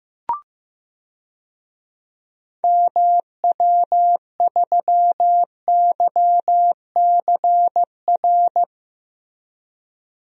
Morse code, heard as a keyed tone, about 15 wpm, 700 Hz. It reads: MW3YCR
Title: DXCC-100 Call Signs - 6 Characters - Repeated 3x - 15wpm